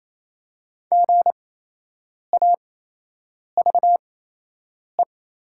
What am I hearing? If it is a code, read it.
Morse code, heard as a keyed tone, about 28 wpm, spaced as for 8 wpm, 700 Hz. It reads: GAVE
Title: 8x Character Spacing - Top 500 Words - 28wpm